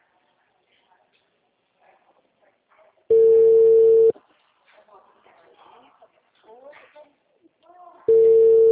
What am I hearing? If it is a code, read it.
Ringback tone (China, 450 Hz), with speech.